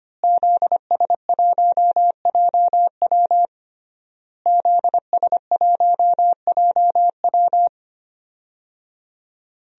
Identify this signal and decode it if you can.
Morse code, heard as a keyed tone, about 25 wpm, 700 Hz. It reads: ZS1JW ZS1JW
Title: DXCC-100 Call Signs - 5 Characters - Repeated 3x - 25wpm